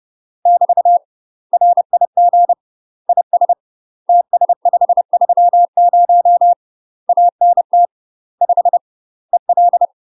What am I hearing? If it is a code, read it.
Morse code, heard as a keyed tone, about 30 wpm, 700 Hz. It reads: <BT> RIG IS TS530 ANT 5 EL
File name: UK MASTER QSOs TRAINING / 30WPM / UK MASTER QSOs 0035